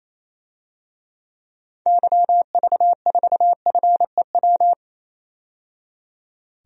Morse code, heard as a keyed tone, about 28 wpm, 700 Hz. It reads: YV4FEW
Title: DXCC-100 Call Signs - 6 Characters - 28wpm